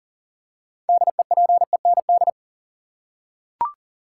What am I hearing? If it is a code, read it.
Morse code, heard as a keyed tone, about 37 wpm, 700 Hz. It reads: DEPEND
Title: Six Letter Words - 40wpm